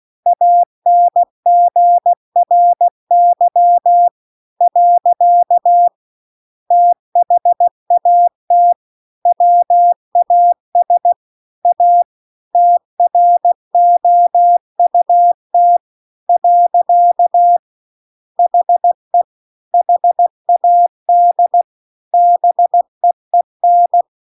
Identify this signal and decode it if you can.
Morse code, heard as a keyed tone, about 16 wpm, 700 Hz. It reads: ANGRY . THAT WAS A TROUT . HE HAD BEEN